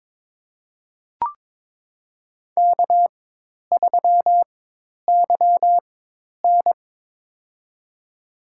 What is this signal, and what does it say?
Morse code, heard as a keyed tone, about 22 wpm, spaced as for 10 wpm, 700 Hz. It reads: K3YN
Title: USA Call Signs - 4x Character Spacing - 22wpm